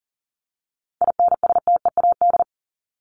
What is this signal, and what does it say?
Morse code, heard as a keyed tone, about 37 wpm, 700 Hz. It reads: INSTEAD
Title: Top 501-750 Words 40wpm